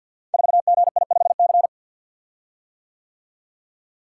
Morse code, heard as a keyed tone, about 49 wpm, 700 Hz. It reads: 4GI5C
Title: Code Groups - 5 Letter:Numbers - 50wpm